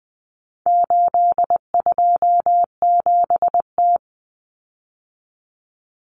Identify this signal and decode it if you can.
Morse code, heard as a keyed tone, about 20 wpm, 700 Hz. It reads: 827T
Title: Common Cut Numbers - 4-Digit 20wpm